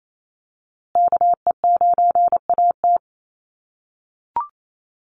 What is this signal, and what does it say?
Morse code, heard as a keyed tone, about 28 wpm, 700 Hz. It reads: KE9AT
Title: USA Call Signs Repeated 2x 28wpm